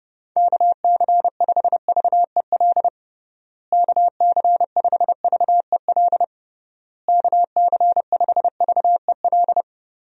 Morse code, heard as a keyed tone, about 32 wpm, 700 Hz. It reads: KC5VEL KC5VEL KC5VEL
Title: USA Call Signs - 6 Characters - Repeated 3x - 30wpm